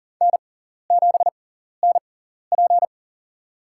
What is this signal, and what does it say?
Morse code, heard as a keyed tone, about 40 wpm, spaced as for 14 wpm, 700 Hz. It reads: N7NP
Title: USA Call Signs - 4 Characters - 6x Character Spacing - 40wpm